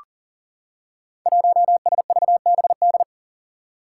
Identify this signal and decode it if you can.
Morse code, heard as a keyed tone, about 40 wpm, 700 Hz. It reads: 1SVBD
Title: Five Challenging Letter-Number Combinations 40wpm